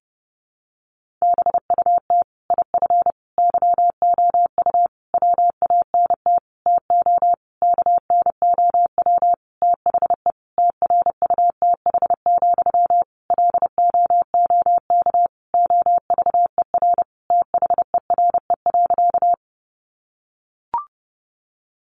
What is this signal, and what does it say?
Morse code, heard as a keyed tone, about 28 wpm, 700 Hz. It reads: BUT IF YOU WANT TO KNOW THE TRUTH, LOOK OVER THERE.
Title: Sentences from Top 800 Words 30wpm